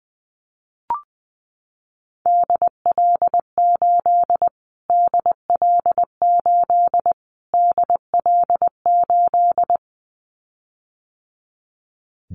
Morse code, heard as a keyed tone, about 20 wpm, 700 Hz. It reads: DL8 DL8 DL8
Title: Callsign Prefixes for Top 25 DXCC - Repeated 3x - 20wpm